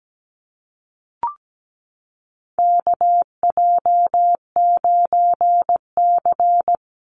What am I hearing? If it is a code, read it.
Morse code, heard as a keyed tone, about 17 wpm, 700 Hz. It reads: KJ9C
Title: Call Signs Repeated 3x - 17wpm